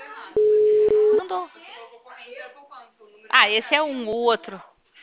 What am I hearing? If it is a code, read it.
Ringback tone (Europe, 425 Hz), with speech.
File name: out- -2012- - - .72345